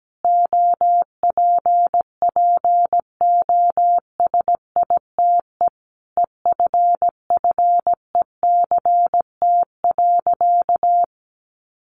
Morse code, heard as a keyed tone, about 17 wpm, 700 Hz. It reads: OPPOSITE EFFECT.
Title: Sentences from Top 1300 Words 17wpm